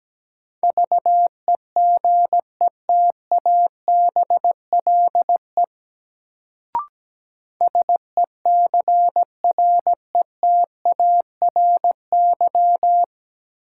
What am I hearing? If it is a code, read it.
Morse code, heard as a keyed tone, about 17 wpm, 700 Hz. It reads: VEGETABLE SECRETARY
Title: Sentences from Top 1300 Words - New Word Drill - 17wpm